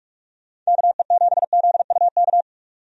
Morse code, heard as a keyed tone, about 46 wpm, 700 Hz. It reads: KE7ZUK
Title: USA Call Signs - 6 Characters - 45wpm